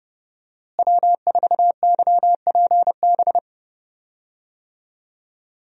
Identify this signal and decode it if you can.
Morse code, heard as a keyed tone, about 30 wpm, 700 Hz. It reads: W4YPB
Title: USA Call Signs - 5 Characters - Repeated 2x - 30wpm